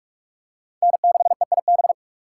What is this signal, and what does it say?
Morse code, heard as a keyed tone, about 44 wpm, 700 Hz. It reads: N6EIB